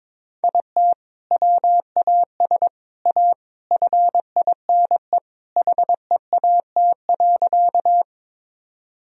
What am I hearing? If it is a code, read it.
Morse code, heard as a keyed tone, about 22 wpm, 700 Hz. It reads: IT WAS A FINE HEAT.
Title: Sentences from Top 1000 Words - Review Part 3 - 22wpm